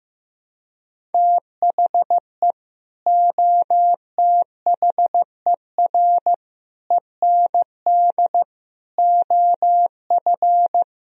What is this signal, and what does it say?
Morse code, heard as a keyed tone, about 15 wpm, 700 Hz. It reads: THE OTHER END OF
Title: Speed-Racing - Sets of 5 Words 15wpm